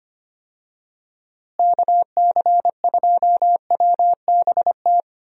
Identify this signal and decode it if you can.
Morse code, heard as a keyed tone, about 25 wpm, 700 Hz. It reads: KC2WBT
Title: USA Call Signs - 6 Characters - Repeated 3x - 25wpm